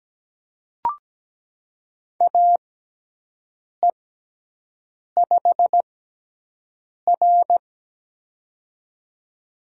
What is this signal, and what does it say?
Morse code, heard as a keyed tone, about 17 wpm, spaced as for 6 wpm, 700 Hz. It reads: AE5R
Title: USA Call Signs - 4 Characters - 6x Character Spacing - 17wpm